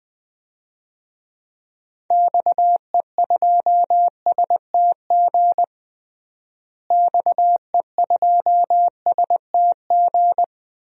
Morse code, heard as a keyed tone, about 20 wpm, 700 Hz. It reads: XE2STG XE2STG
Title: DXCC-100 Call Signs - 6 Characters - Repeated 2x - 20wpm